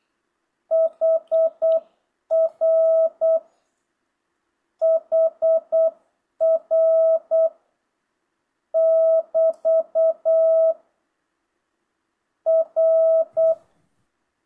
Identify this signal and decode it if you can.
Morse code, heard as a keyed tone, about 8 wpm, 660 Hz. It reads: HR HR <BT> R